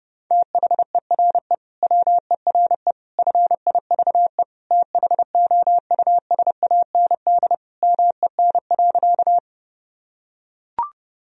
Morse code, heard as a keyed tone, about 30 wpm, 700 Hz. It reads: THERE WERE FIVE THOUSAND MEN.
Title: Sentences from Top 1000 Words - Review Part 3 - 30wpm